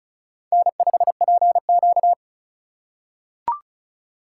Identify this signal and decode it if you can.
Morse code, heard as a keyed tone, about 36 wpm, 700 Hz. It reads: N5PQ
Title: USA Call Signs - 4 Characters - 6x Character Spacing - 35wpm